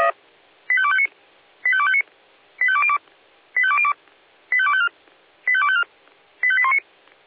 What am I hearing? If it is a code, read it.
Five-tone selective call (ZVEI), heard as quick runs of five tones, tones of about 70 ms. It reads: 85259 85259 85202 85202 85254 85254 85819
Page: /496/SDR/ZVEI/